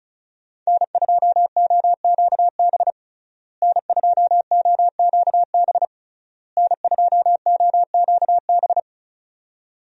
Morse code, heard as a keyed tone, about 35 wpm, 700 Hz. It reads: N2OQB N2OQB N2OQB
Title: USA Call Signs - 5 Characters - Repeated 3x - 35wpm